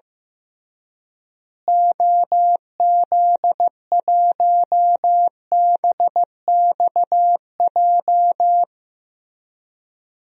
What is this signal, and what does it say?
Morse code, heard as a keyed tone, about 15 wpm, 700 Hz. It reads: OZ1BXJ
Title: DXCC-100 Call Signs - 6 Characters - Repeated 3x - 15wpm